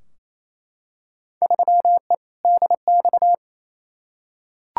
Morse code, heard as a keyed tone, about 28 wpm, 700 Hz. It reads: 3E DX
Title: Field Day Exchanges Encore - 28wpm